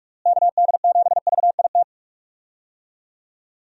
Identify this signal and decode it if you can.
Morse code, heard as a keyed tone, about 44 wpm, 700 Hz. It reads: KD7VIT
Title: USA Call Signs - 6 Characters - Repeated 2x - 45wpm